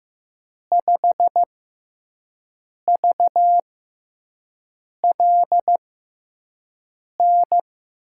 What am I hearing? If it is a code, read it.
Morse code, heard as a keyed tone, about 15 wpm, spaced as for 5 wpm, 700 Hz. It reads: 5VLN
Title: USA Call Signs - 6 Characters - 6x Character Spacing - 15wpm